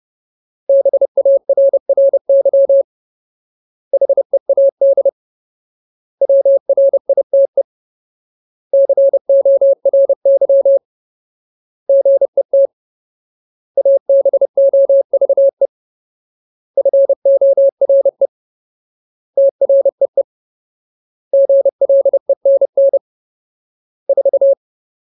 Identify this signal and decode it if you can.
Morse code, heard as a keyed tone, about 30 wpm, 550 Hz. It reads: BARRY HEAD WRITE CORY GET ABOVE FORE TREE GLENN 4
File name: Ultimate Word List / 30WPM Ultimate